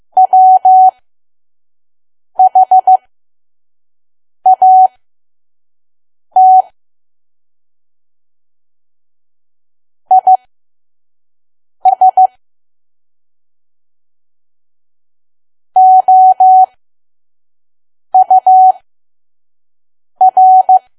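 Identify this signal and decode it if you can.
Morse code, heard as a keyed tone, about 15 wpm, spaced as for 5 wpm, 750 Hz. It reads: WHAT IS OUR
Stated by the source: Morse code practice at 5 WPM from KQ2P.COM.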